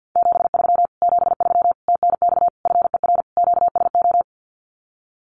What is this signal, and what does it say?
Morse code, heard as a keyed tone, about 50 wpm, 700 Hz. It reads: 73 73 TNX FER QSO
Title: Common QSO Phrases Encore 50wpm